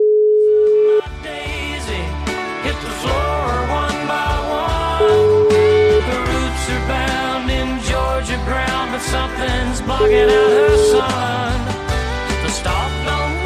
Ringback tone (Europe, 425 Hz), with noise.